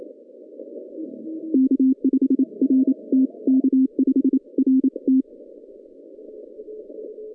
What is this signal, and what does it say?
Morse code, heard as a keyed tone, about 27 wpm, 260 Hz. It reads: K5RT K5RT
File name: 2015 cq ww cw / 2015 cqwww k5tr - 0-27